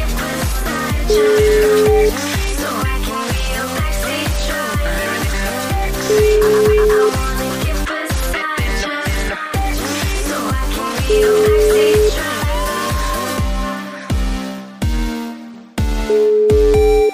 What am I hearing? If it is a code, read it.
Ringback tone (Europe, 425 Hz), with noise.